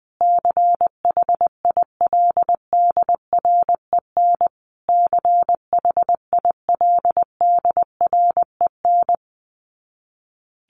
Morse code, heard as a keyed tone, about 20 wpm, 700 Hz. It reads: CHILDREN CHILDREN
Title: Top 300 Words Repeated 20wpm